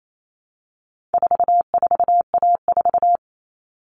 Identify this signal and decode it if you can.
Morse code, heard as a keyed tone, about 28 wpm, 700 Hz. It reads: 44A4
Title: All Cut Numbers - 4-Digit 28wpm